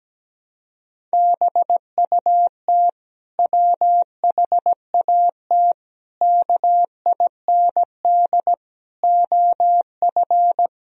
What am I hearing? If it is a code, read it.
Morse code, heard as a keyed tone, about 17 wpm, 700 Hz. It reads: BUT WHAT KIND OF
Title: Sentences from Top 1000 Words - Review Part 1 - 17wpm